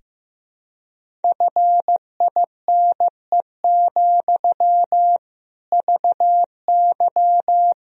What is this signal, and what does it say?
Morse code, heard as a keyed tone, about 16 wpm, 700 Hz. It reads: FINE, VY
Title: Sentences from Top 500 Words and Abbreviations 15wpm